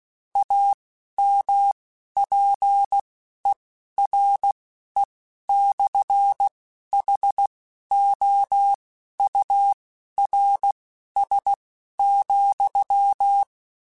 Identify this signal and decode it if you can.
Morse code, heard as a keyed tone, about 16 wpm, spaced as for 11 wpm, 800 Hz. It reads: AMPERE/HOURS,
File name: Technical Subjects / Technical Subjects-mp3s / Sw Pwr Sup @10